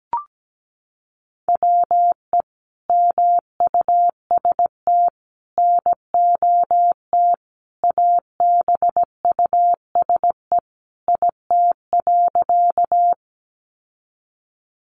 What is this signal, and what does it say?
Morse code, heard as a keyed tone, about 17 wpm, 700 Hz. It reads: WE MUST NOT ABUSE IT.
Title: Sentences from Top 1300 Words 17wpm